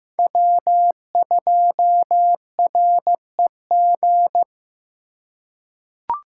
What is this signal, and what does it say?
Morse code, heard as a keyed tone, about 15 wpm, 700 Hz. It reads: W2REG
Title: USA Call Signs - 5 Characters - 3x Character Spacing - 15wpm